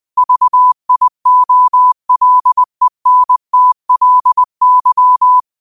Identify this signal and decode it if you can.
Morse code, heard as a keyed tone, about 20 wpm, 1000 Hz. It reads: VIOLENTLY